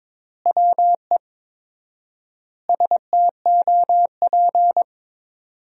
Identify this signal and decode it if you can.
Morse code, heard as a keyed tone, about 22 wpm, 700 Hz. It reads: WE STOP
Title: 4x Word-Spacing - Sentences from Top 300 Words - 22wpm